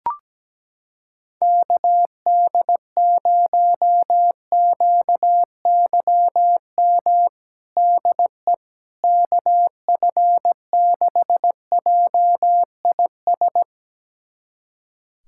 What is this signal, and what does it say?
Morse code, heard as a keyed tone, about 17 wpm, 700 Hz. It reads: KD0QYM DE KF6JIS
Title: USA Callsign DE Callsign - 6 Characters - 17wpm